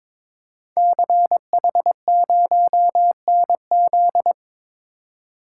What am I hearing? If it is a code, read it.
Morse code, heard as a keyed tone, about 22 wpm, 700 Hz. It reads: CH0NZ
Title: Code Groups - 5 Letter:Numbers - 22wpm